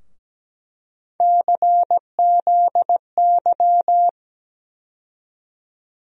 Morse code, heard as a keyed tone, about 17 wpm, 700 Hz. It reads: CZY
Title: USA Call Sign Postfixes - 3 Letters 17wpm